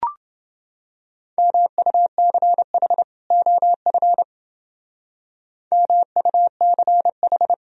Morse code, heard as a keyed tone, about 29 wpm, 700 Hz. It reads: MUCH OF MUCH